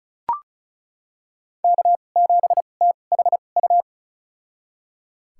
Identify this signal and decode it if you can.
Morse code, heard as a keyed tone, about 36 wpm, spaced as for 25 wpm, 700 Hz. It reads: K7THU
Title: USA Call Signs - 5 Characters - 2x Character Spacing - 35wpm